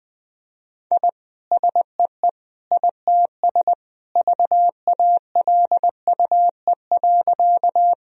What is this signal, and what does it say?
Morse code, heard as a keyed tone, about 21 wpm, 700 Hz. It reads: I SEE ITS VALUE.
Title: Sentences from Top 1000 Words - Review Part 3 - 20wpm